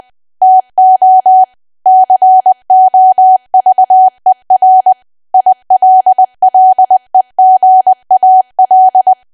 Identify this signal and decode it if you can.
Morse code, heard as a keyed tone, about 20 wpm, 750 Hz. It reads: TO COVER ILLEGAL
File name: cwa / W1AW-Level3 / 20WPM